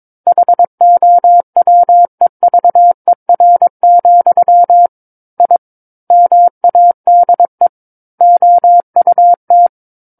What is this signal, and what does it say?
Morse code, heard as a keyed tone, about 22 wpm, 700 Hz. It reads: HOWEVER, I MADE OUT
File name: Book One / 13-Chapter 13 - 22 WPM